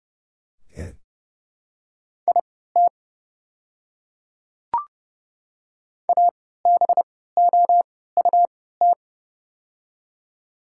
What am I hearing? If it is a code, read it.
Morse code, heard as a keyed tone, about 30 wpm, spaced as for 17 wpm, 700 Hz. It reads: IT ABOUT
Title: 3x Character Spacing - Top 100 Words - 30wpm